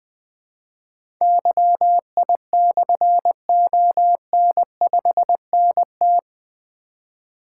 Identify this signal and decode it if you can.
Morse code, heard as a keyed tone, about 20 wpm, 700 Hz. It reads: YI/ON5NT
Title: Difficult Call Signs 20wpm